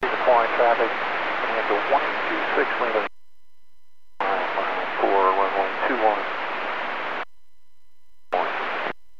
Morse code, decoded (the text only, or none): none